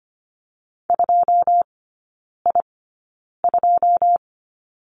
Morse code, heard as a keyed tone, about 25 wpm, spaced as for 9 wpm, 700 Hz. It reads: 2I2
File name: Lesson 15e - Number 5 - Three Characters 25wpm